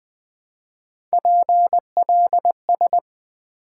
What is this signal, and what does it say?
Morse code, heard as a keyed tone, about 20 wpm, 700 Hz. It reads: PLS
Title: Top 90 QSO Elements - 3x Character Spacing - 20wpm